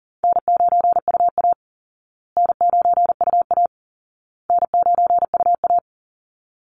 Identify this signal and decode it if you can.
Morse code, heard as a keyed tone, about 40 wpm, 700 Hz. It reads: N9UA N9UA N9UA
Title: Easy Call Signs Repeated 3x - 40wpm